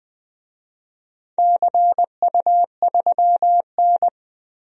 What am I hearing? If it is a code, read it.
Morse code, heard as a keyed tone, about 21 wpm, 700 Hz. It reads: CU3N
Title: DXCC-100 Call Signs - 4 Characters - 20wpm